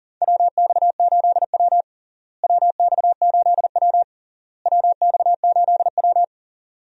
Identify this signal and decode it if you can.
Morse code, heard as a keyed tone, about 40 wpm, 700 Hz. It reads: WX8W WX8W WX8W
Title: USA Call Signs - 4 Characters - Repeated 3x - 40wpm